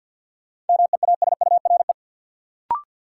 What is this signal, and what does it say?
Morse code, heard as a keyed tone, about 53 wpm, 700 Hz. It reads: MEASURE
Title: Sentences from Top 1000 Words - Review Part 1 - New Word Drill - 50wpm